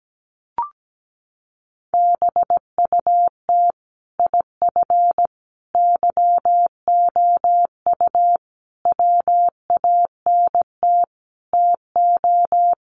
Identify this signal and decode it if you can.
Morse code, heard as a keyed tone, about 17 wpm, 700 Hz. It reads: BUT IF YOU WANT TO
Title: Sentences from Top 800 Words 17wpm